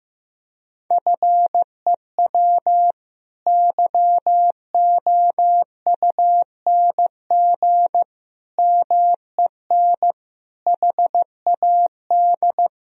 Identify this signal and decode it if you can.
Morse code, heard as a keyed tone, about 15 wpm, 700 Hz. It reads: FEW YOUNG MEN HAD
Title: Sentences from Top 1000 Words - Review Part 1 - 15wpm